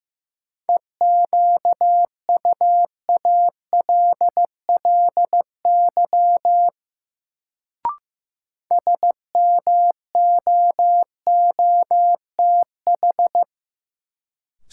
Morse code, heard as a keyed tone, about 15 wpm, 700 Hz. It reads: EQUALLY SMOOTH
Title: Sentences from Top 1400 Words - New Word Drill - 15wpm